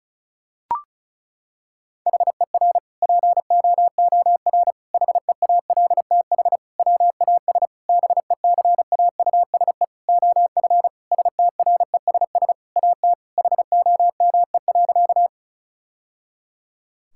Morse code, heard as a keyed tone, about 36 wpm, 700 Hz. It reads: HER POOR HEALTH WAS BECAUSE OF STRESS AT HOME.
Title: Sentences from Top 1000 Words - Review Part 6 - 35wpm